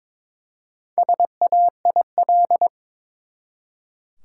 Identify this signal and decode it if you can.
Morse code, heard as a keyed tone, about 22 wpm, 700 Hz. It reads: SAIL
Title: Four Letter Words - 22wpm